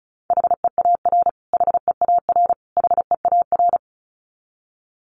Morse code, heard as a keyed tone, about 34 wpm, 700 Hz. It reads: HEAR HEAR HEAR
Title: Top 500 Words Repeated 35wpm